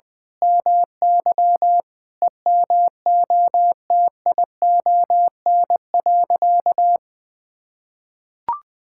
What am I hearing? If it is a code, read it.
Morse code, heard as a keyed tone, about 20 wpm, 700 Hz. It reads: MY EMOTION.
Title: Sentences from Top 1600 Words 20wpm